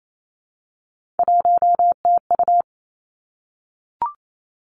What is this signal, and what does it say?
Morse code, heard as a keyed tone, about 28 wpm, 700 Hz. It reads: 1TU
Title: Three Challenging Letter-Number Combinations 28wpm